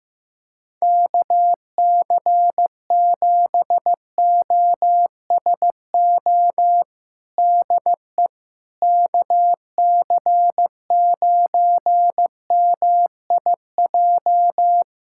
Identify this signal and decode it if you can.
Morse code, heard as a keyed tone, about 15 wpm, 700 Hz. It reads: KC7OSO DE KC9MIJ
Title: USA Callsign DE Callsign - 6 Characters - 15wpm